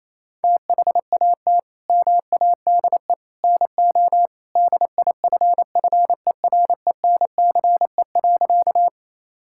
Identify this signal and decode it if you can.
Morse code, heard as a keyed tone, about 28 wpm, 700 Hz. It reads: THAT MADE NO DIFFERENCE.
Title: Sentences from Top 1000 Words - Review Part 5 - 28wpm